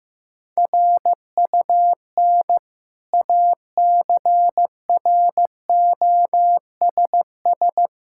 Morse code, heard as a keyed tone, about 15 wpm, 700 Hz. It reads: RUN ACROSS
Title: Sentences from Top 1000 Words - Review Part 5 - 15wpm